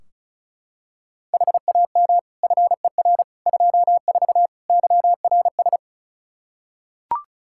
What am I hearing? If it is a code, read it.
Morse code, heard as a keyed tone, about 36 wpm, 700 Hz. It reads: HAM FER 24 YRS
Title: Practice QSO - Exchange 3 - More Info - 35wpm